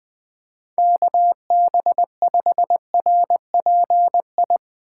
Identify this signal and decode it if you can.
Morse code, heard as a keyed tone, about 20 wpm, 700 Hz. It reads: KB5RPI
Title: Speed Racing - USA 2x3 Call Signs 17wpm